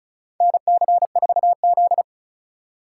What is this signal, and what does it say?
Morse code, heard as a keyed tone, about 36 wpm, 700 Hz. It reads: NC4Z
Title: USA Call Signs - 4 Characters - 2x Character Spacing - 35wpm